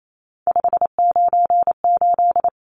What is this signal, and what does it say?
Morse code, heard as a keyed tone, about 28 wpm, 700 Hz. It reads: 598
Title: Numbers - 3-Digit 28wpm